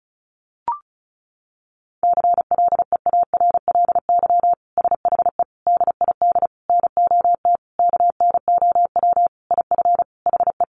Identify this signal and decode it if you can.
Morse code, heard as a keyed tone, about 34 wpm, 700 Hz. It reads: CLEARLY SHE DID NOT KNOW IF HE
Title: Sentences from Top 1300 Words 35wpm